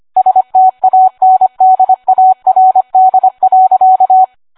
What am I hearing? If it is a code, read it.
Morse code, heard as a keyed tone, about 25 wpm, 750 Hz. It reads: STANDARD.